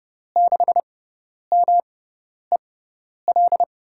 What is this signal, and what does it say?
Morse code, heard as a keyed tone, about 31 wpm, spaced as for 10 wpm, 700 Hz. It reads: 6MEL
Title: USA Call Signs - 6 Characters - 6x Character Spacing - 30wpm